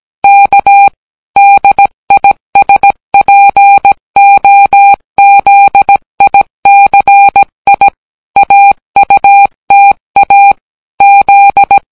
Morse code, heard as a keyed tone, about 17 wpm, 800 Hz. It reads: K DISPOZICI AUTA Z